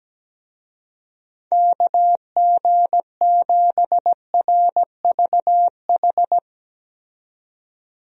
Morse code, heard as a keyed tone, about 17 wpm, 700 Hz. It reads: KG7RVH
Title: USA Call Signs - 6 Characters - 17wpm